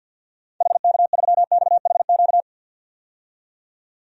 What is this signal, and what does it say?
Morse code, heard as a keyed tone, about 52 wpm, 700 Hz. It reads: HK3XHQ
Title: DXCC-100 Call Signs - 6 Characters - 50wpm